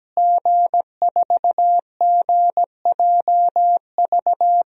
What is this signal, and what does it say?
Morse code, heard as a keyed tone, about 17 wpm, 700 Hz. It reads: G4GJV
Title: DXCC-100 Call Signs - 5 Characters - Repeated 3x - 17wpm